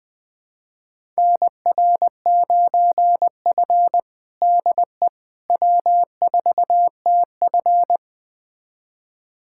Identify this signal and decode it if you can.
Morse code, heard as a keyed tone, about 20 wpm, 700 Hz. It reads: NR9F DE W4TF
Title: USA Callsign DE Callsign - 4 Characters - 20wpm